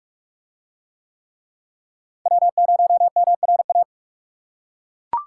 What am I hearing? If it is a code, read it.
Morse code, heard as a keyed tone, about 42 wpm, 700 Hz. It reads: W0MRA